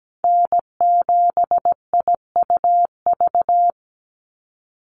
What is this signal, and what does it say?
Morse code, heard as a keyed tone, about 17 wpm, 700 Hz. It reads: N7IUV
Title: USA Call Signs 17wpm